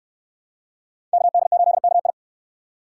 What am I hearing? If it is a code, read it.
Morse code, heard as a keyed tone, about 69 wpm, 700 Hz. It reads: KD8KI